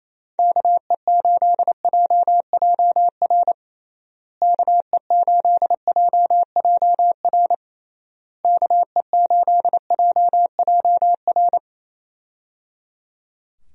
Morse code, heard as a keyed tone, about 27 wpm, 700 Hz. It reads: KE8JJR KE8JJR KE8JJR
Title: USA Call Signs - 6 Characters - Repeated 3x - 28wpm